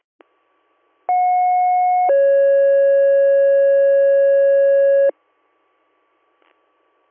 Two-tone page, heard as a one-second tone, then a three-second tone, about 730 then 550 Hz. It is heard from 1.1 to 5.1 s.